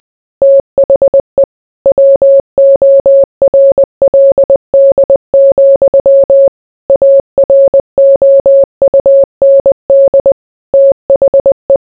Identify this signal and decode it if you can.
Morse code, heard as a keyed tone, about 20 wpm, 550 Hz. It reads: THE WORLD, AROUND THE